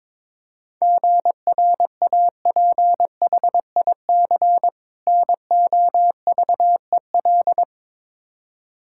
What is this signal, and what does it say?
Morse code, heard as a keyed tone, about 22 wpm, 700 Hz. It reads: GRAPHIC NOVEL
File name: Difficult Words with Hints - 22wpm